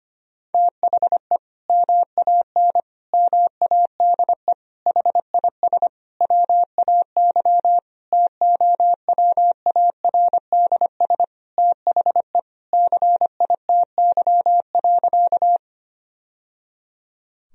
Morse code, heard as a keyed tone, about 25 wpm, 700 Hz. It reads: THE MAN MADE HIS WAY TOWARDS THE CITY.
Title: Sentences from Top 1000 Words - Review Part 2 - 25wpm